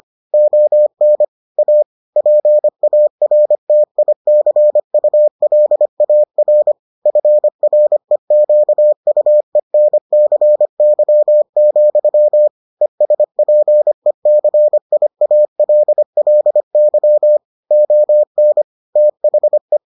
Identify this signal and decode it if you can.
Morse code, heard as a keyed tone, about 25 wpm, 600 Hz. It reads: ON A PARTICULAR FREQUENCY, ESPECIALLY ON THE